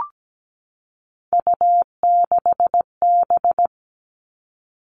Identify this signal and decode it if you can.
Morse code, heard as a keyed tone, about 17 wpm, 700 Hz. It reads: U6B
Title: All Cut Numbers - 3-Digit 17wpm